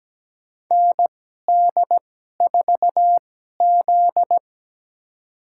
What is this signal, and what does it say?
Morse code, heard as a keyed tone, about 17 wpm, spaced as for 12 wpm, 700 Hz. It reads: ND4Z
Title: USA Call Signs - 4 Characters - 2x Character Spacing - 17wpm